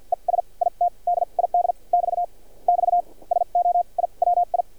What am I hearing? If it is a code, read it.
Morse code, heard as a keyed tone, about 50 wpm, spaced as for 40 wpm, 700 Hz. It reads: ES IT DID <BT> <BT> SO I WI
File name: archived / CRSnet